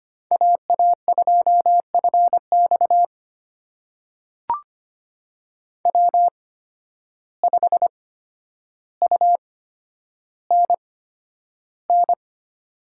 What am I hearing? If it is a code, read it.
Morse code, heard as a keyed tone, about 25 wpm, spaced as for 11 wpm, 700 Hz. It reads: AA2FX W5UNN
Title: USA Call Signs - 5 Characters - 8x Character Spacing - 25wpm